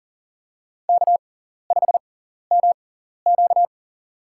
Morse code, heard as a keyed tone, about 40 wpm, spaced as for 14 wpm, 700 Hz. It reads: K5MQ
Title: USA Call Signs - 4 Characters - 6x Character Spacing - 40wpm